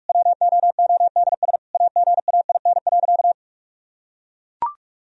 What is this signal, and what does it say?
Morse code, heard as a keyed tone, about 44 wpm, 700 Hz. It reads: WOODS AGAIN.
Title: Sentences from Top 1000 Words - Review Part 2 - 45wpm